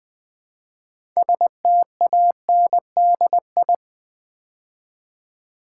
Morse code, heard as a keyed tone, about 20 wpm, 700 Hz. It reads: STANDI
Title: Build A Word - 3 Letters At A Time - 8-Letter Words - 20wpm